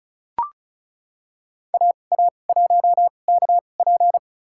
Morse code, heard as a keyed tone, about 34 wpm, spaced as for 25 wpm, 700 Hz. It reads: AA1KP
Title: USA Call Signs - 5 Characters - 2x Character Spacing - 35wpm